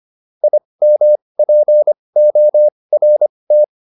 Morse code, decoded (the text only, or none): IMPORT